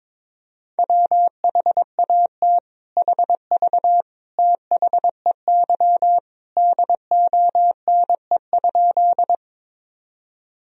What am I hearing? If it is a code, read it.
Morse code, heard as a keyed tone, about 22 wpm, 700 Hz. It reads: WHAT HV THEY DONE?
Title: Sentences from Top 500 Words and Abbreviations 22wpm